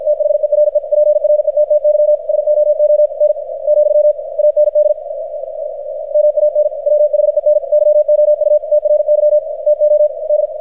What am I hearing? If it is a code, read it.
Morse code, heard as a keyed tone, about 53 wpm, 600 Hz, with radio noise. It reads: THERE PRETTY SOON , AND MAN WHEN YOU TRY TO D
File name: archived / CRSnet